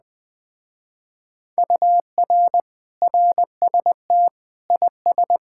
Morse code, heard as a keyed tone, about 20 wpm, 700 Hz. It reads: UR RST IS
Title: Speed-Racing Common QSO Phrases Encore - 17wpm